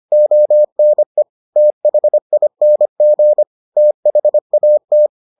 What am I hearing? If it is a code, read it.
Morse code, heard as a keyed tone, about 25 wpm, 600 Hz. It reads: ONE THING THAT